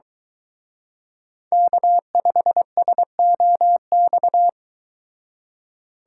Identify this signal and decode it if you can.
Morse code, heard as a keyed tone, about 23 wpm, 700 Hz. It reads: K5SOX